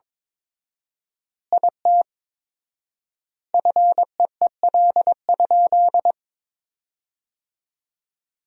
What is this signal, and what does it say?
Morse code, heard as a keyed tone, about 22 wpm, 700 Hz. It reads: IT FEEL?
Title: 4x Word-Spacing - Sentences from Top 400 Words - 22wpm